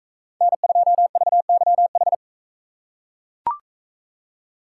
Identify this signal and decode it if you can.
Morse code, heard as a keyed tone, about 44 wpm, 700 Hz. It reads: N2VYH